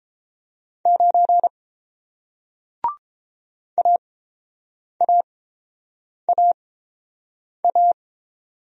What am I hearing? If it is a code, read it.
Morse code, heard as a keyed tone, about 31 wpm, spaced as for 7 wpm, 700 Hz. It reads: 9 AAAA